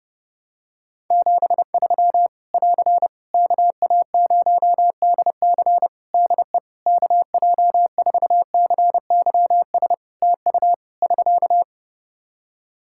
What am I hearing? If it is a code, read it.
Morse code, heard as a keyed tone, about 32 wpm, 700 Hz. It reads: 73 <AR> KA0DC DE KJ4CYS TU <SK>
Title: K1USN SST QSO Practice 30wpm